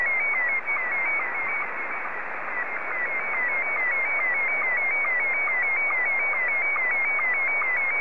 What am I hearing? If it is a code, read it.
SSTV (slow-scan television), Scottie 1.